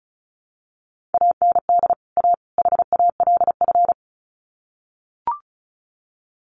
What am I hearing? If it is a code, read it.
Morse code, heard as a keyed tone, about 34 wpm, 700 Hz. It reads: AND A HALF
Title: Sets of 3 Words 35wpm